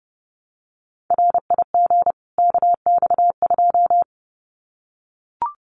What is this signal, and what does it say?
Morse code, heard as a keyed tone, about 30 wpm, 700 Hz. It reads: RIG KX2
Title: Common QSO Phrases 30wpm